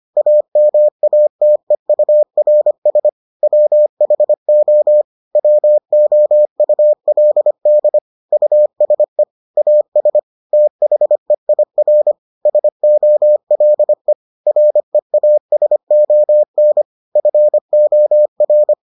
Morse code, heard as a keyed tone, about 25 wpm, 600 Hz. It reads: AMATEURS WHO WOULD USE AS THEIR SOLE REASON FOR